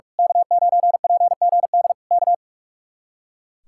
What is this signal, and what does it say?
Morse code, heard as a keyed tone, about 43 wpm, 700 Hz. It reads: K9PGD K
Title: Speed-Racing Common QSO Phrases - 45wpm